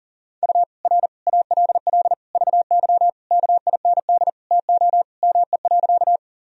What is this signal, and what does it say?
Morse code, heard as a keyed tone, about 40 wpm, 700 Hz. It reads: U R ALL VY KIND TO ME.
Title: Sentences from Top 200 Words and Abbreviations 40wpm